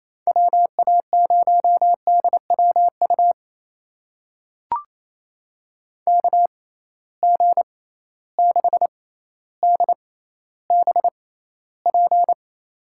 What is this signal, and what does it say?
Morse code, heard as a keyed tone, about 28 wpm, spaced as for 14 wpm, 700 Hz. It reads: WA0DWU KG6DBP
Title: USA Call Signs - 6 Characters - 6x Character Spacing - 28wpm